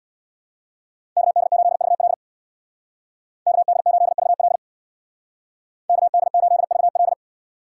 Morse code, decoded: KD8VL KD8VL KD8VL

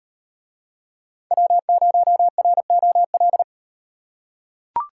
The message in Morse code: W0ROL

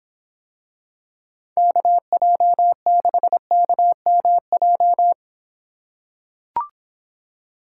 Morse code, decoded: KJ6KMJ

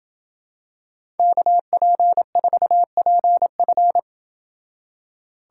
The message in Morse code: KP4PF

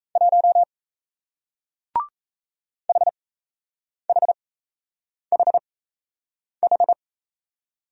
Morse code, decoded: 1 HHHH

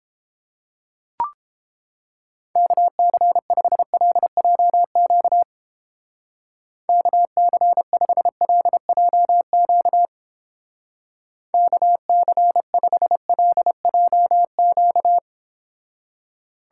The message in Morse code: KC5LJQ KC5LJQ KC5LJQ

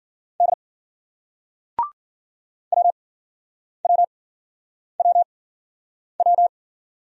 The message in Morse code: D WWWW

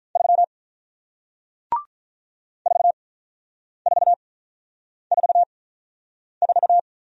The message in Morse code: 3 4444

